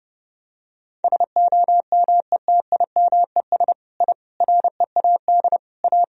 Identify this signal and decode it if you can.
Morse code, heard as a keyed tone, about 31 wpm, 700 Hz. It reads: SOMETIMES I READ A
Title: Sentences from Top 1000 Words - Review Part 3 - 30wpm